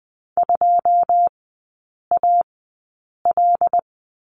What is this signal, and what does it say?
Morse code, heard as a keyed tone, about 20 wpm, spaced as for 8 wpm, 700 Hz. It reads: 2AL
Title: 159 Lesson 26e - Letter g - Three Characters 20wpm